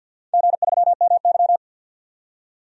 Morse code, heard as a keyed tone, about 47 wpm, 700 Hz. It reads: G3MY
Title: DXCC-100 Call Signs - 4 Characters - 50wpm